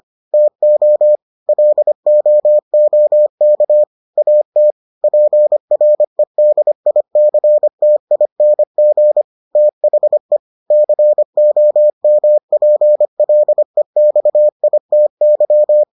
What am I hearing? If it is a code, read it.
Morse code, heard as a keyed tone, about 25 wpm, 600 Hz. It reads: TO LOOK AT PREDICTING THE COMPLEXITY